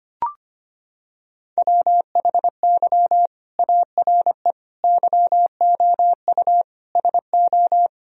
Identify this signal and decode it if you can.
Morse code, heard as a keyed tone, about 25 wpm, 700 Hz. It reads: WHY ARE YOU SO
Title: Sentences from Top 1000 Words - Review Part 3 - 25wpm